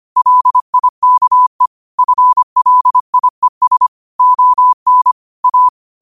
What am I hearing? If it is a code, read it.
Morse code, heard as a keyed tone, about 25 wpm, 1000 Hz. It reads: LIKE FLIES ON A
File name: qotdfiles / 25wpm